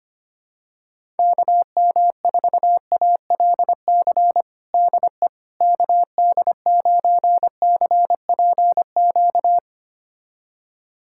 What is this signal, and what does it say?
Morse code, heard as a keyed tone, about 25 wpm, 700 Hz. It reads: KM4ALC DE KD9CPQ
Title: USA Callsign DE Callsign 25wpm